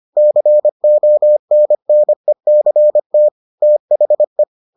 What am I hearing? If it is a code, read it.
Morse code, heard as a keyed tone, about 25 wpm, 600 Hz. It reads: CONNECT THE